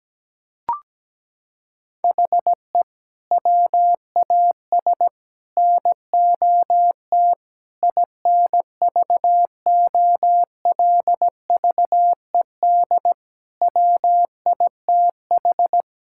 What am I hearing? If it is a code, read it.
Morse code, heard as a keyed tone, about 17 wpm, 700 Hz. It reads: HE WAS NOT INVOLVED WITH
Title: Sentences from Top 1000 Words - Review Part 5 - 17wpm